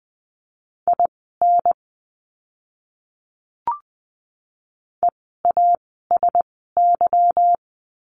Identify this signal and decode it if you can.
Morse code, heard as a keyed tone, about 20 wpm, spaced as for 14 wpm, 700 Hz. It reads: IN EASY
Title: 146 Lesson 24f - Letter y - Words 20wpm